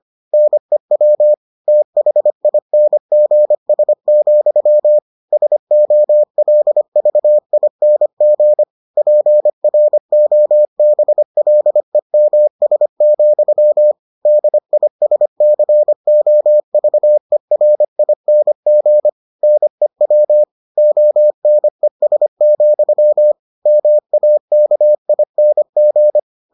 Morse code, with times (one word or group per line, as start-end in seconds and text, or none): 0.3-1.3 NEW
1.7-5.0 THINGS,
5.3-8.6 SOLVING
9.0-13.9 PROBLEMS,
14.3-19.1 DISCOVERING
19.4-20.4 NEW
20.8-23.3 ONES,
23.7-26.2 MAKING